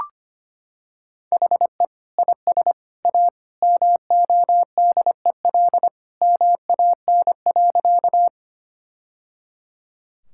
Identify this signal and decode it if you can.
Morse code, heard as a keyed tone, about 25 wpm, 700 Hz. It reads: HE IS A MODEL MAN.
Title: Sentences from Top 1000 Words - Review Part 5 - 25wpm